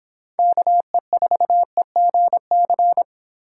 Morse code, heard as a keyed tone, about 26 wpm, 700 Hz. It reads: KE4EGC